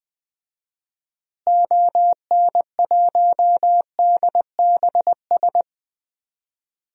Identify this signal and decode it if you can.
Morse code, heard as a keyed tone, about 20 wpm, 700 Hz. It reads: ON1DBS
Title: DXCC-100 Call Signs - 6 Characters - Repeated 2x - 20wpm